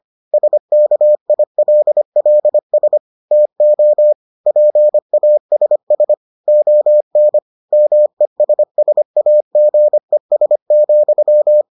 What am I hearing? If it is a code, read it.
Morse code, heard as a keyed tone, about 25 wpm, 600 Hz. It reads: SKILLS TO PASS ON MESSAGES,